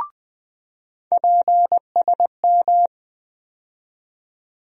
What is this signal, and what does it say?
Morse code, heard as a keyed tone, about 20 wpm, 700 Hz. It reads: PSM
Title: USA Call Sign Postfixes - 3 Letters 20wpm